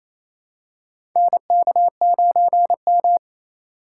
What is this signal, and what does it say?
Morse code, heard as a keyed tone, about 28 wpm, 700 Hz. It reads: NK9M
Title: USA Call Signs - 4 Characters - 2x Character Spacing - 28wpm